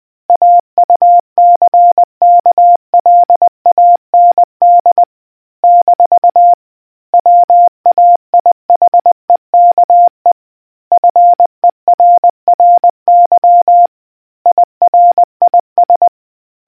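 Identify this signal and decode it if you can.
Morse code, heard as a keyed tone, about 20 wpm, 700 Hz. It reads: AUCKLAND - WAIHEKE FERRY IRIS
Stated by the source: Morse quiz question from a New Zealand navy communicators association.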